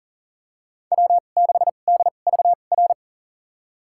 Morse code, heard as a keyed tone, about 40 wpm, spaced as for 29 wpm, 700 Hz. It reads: W6DVR
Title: USA Call Signs - 2x Character Spacing - 40wpm